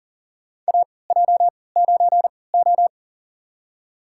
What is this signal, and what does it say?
Morse code, heard as a keyed tone, about 40 wpm, spaced as for 23 wpm, 700 Hz. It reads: AJ9O
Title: USA Call Signs - 4 Characters - 3x Character Spacing - 40wpm